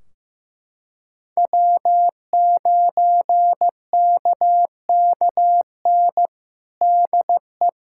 Morse code, heard as a keyed tone, about 16 wpm, 700 Hz. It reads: W9KKN DE
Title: Practice QSO - Exchange 4 - Ending - 15wpm